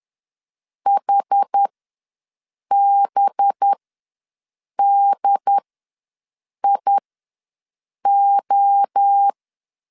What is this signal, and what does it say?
Morse code, heard as a keyed tone, about 10 wpm, spaced as for 6 wpm, 800 Hz. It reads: HBDIO
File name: LEZIONI / 5 LBVC / 5 LBV ws 10wpm 3645